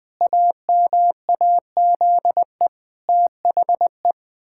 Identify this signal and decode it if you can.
Morse code, heard as a keyed tone, about 19 wpm, 700 Hz. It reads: AMAZE THE
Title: Pangram Challenge v2 - 20wpm